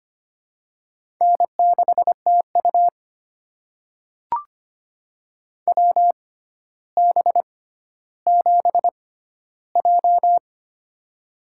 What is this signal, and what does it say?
Morse code, heard as a keyed tone, about 25 wpm, spaced as for 13 wpm, 700 Hz. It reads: N6TU WB7J
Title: USA Call Signs - 4 Characters - 6x Character Spacing - 25wpm